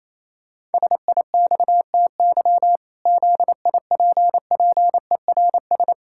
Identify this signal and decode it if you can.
Morse code, heard as a keyed tone, about 28 wpm, 700 Hz. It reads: SIXTY ZIPPERS
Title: Pangram Challenge v2 - 28wpm